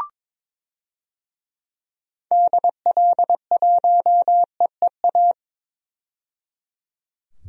Morse code, heard as a keyed tone, about 22 wpm, 700 Hz. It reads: DL1EEA
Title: DXCC-100 Call Signs - 6 Characters - 22wpm